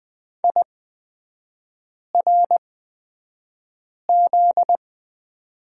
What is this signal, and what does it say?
Morse code, heard as a keyed tone, about 20 wpm, spaced as for 5 wpm, 700 Hz. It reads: IRZ
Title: 243 Lesson 39e - Letter z - Three Characters 20wpm